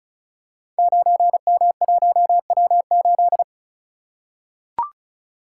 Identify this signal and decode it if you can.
Morse code, heard as a keyed tone, about 34 wpm, 690 Hz. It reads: 9M1W8
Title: Code Groups - 5 Letter:Numbers - 35wpm